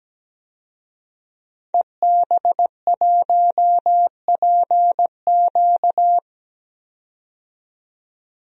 Morse code, heard as a keyed tone, about 17 wpm, 700 Hz. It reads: EB1PQ
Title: DXCC-100 Call Signs - 5 Characters - 17wpm